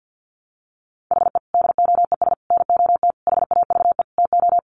Morse code, heard as a keyed tone, about 50 wpm, 700 Hz. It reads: HE DOES NOT HAVE TO
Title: Sets of 5 Words Encore 50wpm